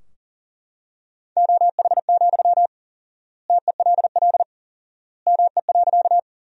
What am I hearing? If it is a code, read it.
Morse code, heard as a keyed tone, about 40 wpm, 700 Hz. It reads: OH, TELL ME.
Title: 4x Word-Spacing - Sentences from Top 500 Words - 40wpm